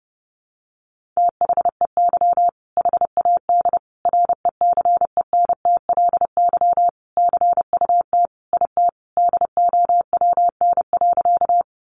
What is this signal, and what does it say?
Morse code, heard as a keyed tone, about 30 wpm, 700 Hz. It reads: THEY HAD RECENTLY CUT IT DOWN.
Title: Sentences from Top 1200 Words 30wpm